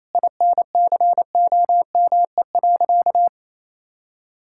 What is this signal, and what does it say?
Morse code, heard as a keyed tone, about 28 wpm, 700 Hz. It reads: INCOME.
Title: Sentences from Top 1000 Words - Review Part 6 - 28wpm